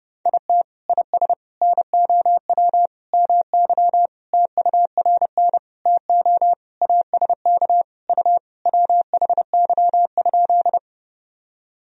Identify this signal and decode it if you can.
Morse code, heard as a keyed tone, about 30 wpm, 700 Hz. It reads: IT IS NOW MY TURN TO ASK U WHY?
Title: Sentences from Top 200 Words and Abbreviations 30wpm